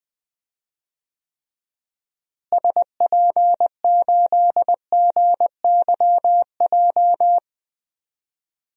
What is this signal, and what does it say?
Morse code, heard as a keyed tone, about 20 wpm, 700 Hz. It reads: SP8GYJ
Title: DXCC-100 Call Signs - 6 Characters - Repeated 2x - 20wpm